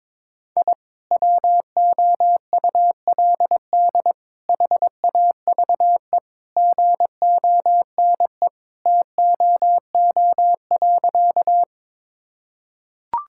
Morse code, heard as a keyed tone, about 21 wpm, 700 Hz. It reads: I WOULD HAVE GONE TOO.
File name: practice / 22wpm / Sentences with 2-Word Phrases - 22wpm